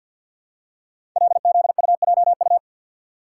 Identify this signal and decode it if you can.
Morse code, heard as a keyed tone, about 52 wpm, 700 Hz. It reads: LZUJU
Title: Code Groups - 5 Letters - 50wpm